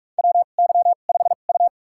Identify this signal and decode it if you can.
Morse code, heard as a keyed tone, about 46 wpm, spaced as for 32 wpm, 700 Hz. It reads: WY5U
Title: USA Call Signs - 4 Characters - 2x Character Spacing - 45wpm